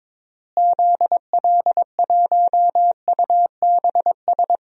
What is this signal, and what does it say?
Morse code, heard as a keyed tone, about 22 wpm, 700 Hz. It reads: ZL1UBS
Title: DXCC-100 Call Signs - 6 Characters - Repeated 3x - 22wpm